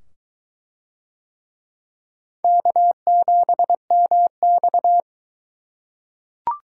Morse code, heard as a keyed tone, about 21 wpm, 700 Hz. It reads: K7MX